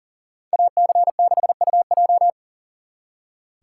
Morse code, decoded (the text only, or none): AC6UJ